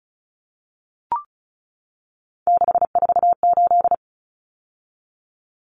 Morse code, heard as a keyed tone, about 34 wpm, 700 Hz. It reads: 648
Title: Numbers - 3-Digit 35wpm